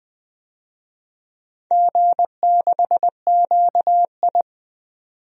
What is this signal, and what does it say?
Morse code, heard as a keyed tone, about 20 wpm, 700 Hz. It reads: G6QI
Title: DXCC-100 Call Signs - 4 Characters - 20wpm